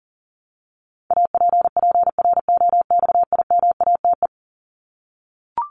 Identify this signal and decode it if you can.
Morse code, heard as a keyed tone, about 40 wpm, 700 Hz. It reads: APPROXIMATE
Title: LY Suffix - Top 100 Words with Base Words 40wpm